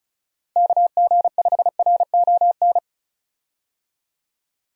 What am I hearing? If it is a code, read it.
Morse code, heard as a keyed tone, about 34 wpm, 700 Hz. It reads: KG5RON
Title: USA Call Signs - 6 Characters - 35wpm